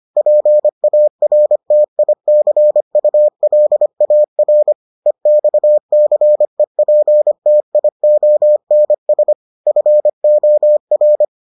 Morse code, heard as a keyed tone, about 25 wpm, 600 Hz. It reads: PARTICULAR EXCEPTIONS FOR